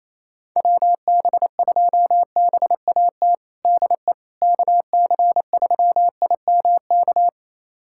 Morse code, heard as a keyed tone, about 28 wpm, 700 Hz. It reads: WB2BAT DE KC3IMK
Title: USA Callsign DE Callsign - 6 Characters - 28wpm